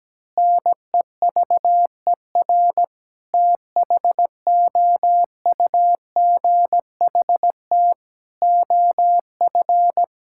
Morse code, heard as a keyed tone, about 17 wpm, 700 Hz. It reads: NEVER THOUGHT OF
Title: Sentences from Top 1600 Words 17wpm